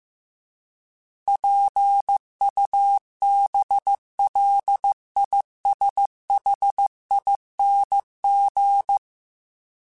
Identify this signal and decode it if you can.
Morse code, heard as a keyed tone, about 15 wpm, 800 Hz. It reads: PUBLISHING